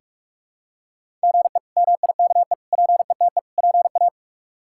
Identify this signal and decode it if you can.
Morse code, heard as a keyed tone, about 46 wpm, 700 Hz. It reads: GE MIKE PETE PA
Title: K1USN SST QSO Practice - 45wpm